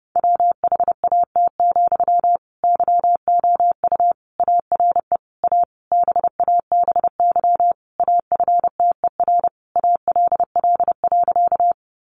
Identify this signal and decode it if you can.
Morse code, heard as a keyed tone, about 30 wpm, 700 Hz. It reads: WHAT, YOU ARE A BABY AFTER ALL.
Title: Sentences from Top 1300 Words 30wpm